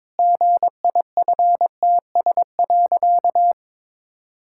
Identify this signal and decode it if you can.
Morse code, heard as a keyed tone, about 22 wpm, 700 Hz. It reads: GIFTS.
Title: Pangram Challenge v2 - 22wpm